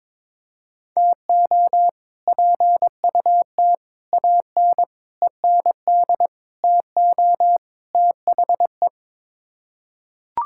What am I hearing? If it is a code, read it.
Morse code, heard as a keyed tone, about 22 wpm, 700 Hz. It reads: TO PUT AN END TO THE
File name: practice / 22wpm / Sets of 6 Words - 22wpm